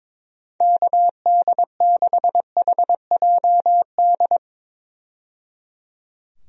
Morse code, heard as a keyed tone, about 22 wpm, 700 Hz. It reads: KD6HJD
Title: USA Call Signs - 6 Characters - 22wpm